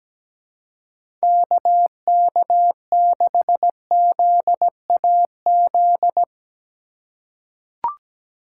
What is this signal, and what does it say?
Morse code, heard as a keyed tone, about 17 wpm, 700 Hz. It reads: KK6ZAZ
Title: USA Call Signs - 6 Characters - 3x Character Spacing - 17wpm